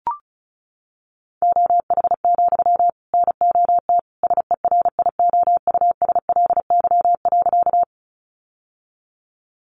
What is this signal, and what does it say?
Morse code, heard as a keyed tone, about 34 wpm, 700 Hz. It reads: OH, NOT SERIOUSLY.
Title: Sentences from Top 1200 Words 35wpm